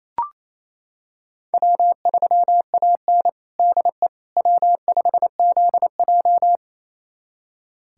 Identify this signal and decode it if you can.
Morse code, heard as a keyed tone, about 27 wpm, 700 Hz. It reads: W3AN DE W5ZJ
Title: USA Callsign DE Callsign - 4 Characters - 28wpm